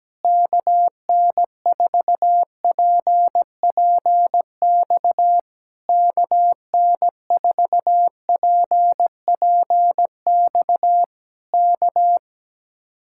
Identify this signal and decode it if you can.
Morse code, heard as a keyed tone, about 17 wpm, 700 Hz. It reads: KN4PPX KN4PPX K
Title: Speed-Racing Common QSO Phrases - 17wpm